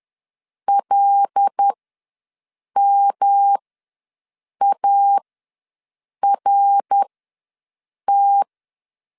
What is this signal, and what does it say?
Morse code, heard as a keyed tone, about 10 wpm, spaced as for 6 wpm, 800 Hz. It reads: LMART
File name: LEZIONI / 6 PKX0 / 6 PKX ws 10wpm 3706